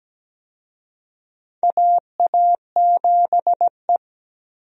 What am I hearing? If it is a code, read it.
Morse code, heard as a keyed tone, about 17 wpm, 700 Hz. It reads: AA7E
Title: USA Call Signs - 4 Characters - 17wpm